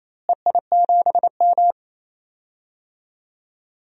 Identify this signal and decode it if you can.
Morse code, heard as a keyed tone, about 28 wpm, 700 Hz. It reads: EI7M
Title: DXCC-100 Call Signs - 4 Characters - Repeated 3x - 28wpm